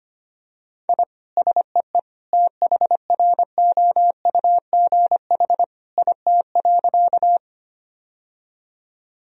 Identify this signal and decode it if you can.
Morse code, heard as a keyed tone, about 25 wpm, 700 Hz. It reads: I SEE THROUGH IT.
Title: Sentences from Top 1000 Words - Review Part 1 - 25wpm